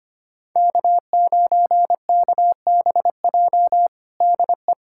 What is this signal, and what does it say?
Morse code, heard as a keyed tone, about 25 wpm, 700 Hz. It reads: K9KBJ DE
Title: USA Callsign DE Callsign - 5 Characters - 25wpm